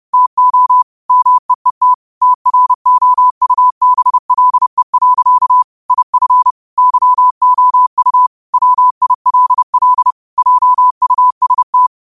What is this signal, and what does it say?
Morse code, heard as a keyed tone, about 30 wpm, 1000 Hz. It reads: TO MEET TROUBLE. IF YOU WILL JUST
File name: qotdfiles / 30wpm